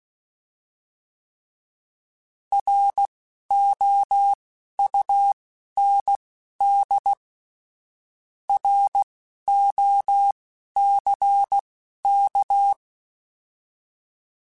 Morse code, heard as a keyed tone, about 16 wpm, spaced as for 11 wpm, 800 Hz. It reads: ROUND ROCK